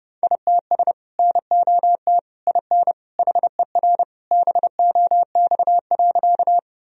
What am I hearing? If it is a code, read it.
Morse code, heard as a keyed tone, about 32 wpm, 700 Hz. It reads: ITS NOT IN HER BOX.
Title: Sentences from Top 1000 Words - Review Part 2 - 30wpm